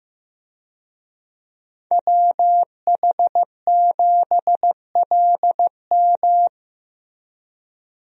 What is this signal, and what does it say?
Morse code, heard as a keyed tone, about 15 wpm, 700 Hz. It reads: WH7LM
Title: USA Call Signs - 5 Characters - Repeated 3x - 15wpm